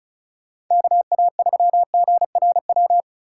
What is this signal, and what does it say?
Morse code, heard as a keyed tone, about 40 wpm, 700 Hz. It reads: KA3GRW